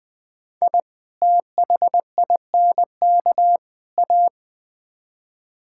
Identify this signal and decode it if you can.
Morse code, heard as a keyed tone, about 21 wpm, 700 Hz. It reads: I THINK A
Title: Build A Phrase - 5-Word Phrases - 20wpm